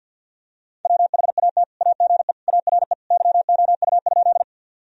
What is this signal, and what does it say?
Morse code, heard as a keyed tone, about 49 wpm, 700 Hz. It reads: WHAT AGE ARE YOU?
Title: Sentences from Top 1000 Words - Review Part 3 - 50wpm